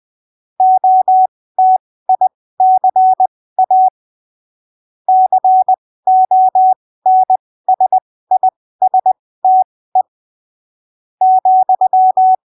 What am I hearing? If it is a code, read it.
Morse code, heard as a keyed tone, about 20 wpm, spaced as for 15 wpm, 750 Hz. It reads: OTICA CONSISTE ,
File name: arquivos / texto / texto 11 20ppm